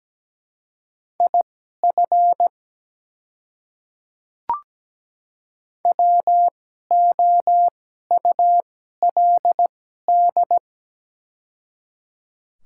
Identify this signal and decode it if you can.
Morse code, heard as a keyed tone, about 17 wpm, spaced as for 12 wpm, 700 Hz. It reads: IF WOULD
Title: 2x Character Spacing - Top 200 Words - 17wpm